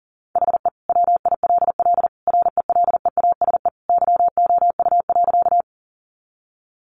Morse code, heard as a keyed tone, about 40 wpm, 700 Hz. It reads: HE WILL RELEASE YOU.
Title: Sentences from Top 1100 Words 40wpm